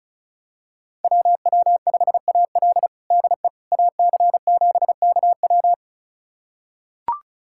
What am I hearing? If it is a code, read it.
Morse code, heard as a keyed tone, about 34 wpm, 700 Hz. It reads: WW5AL DE AC7KW
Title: USA Callsign DE Callsign - 5 Characters - 35wpm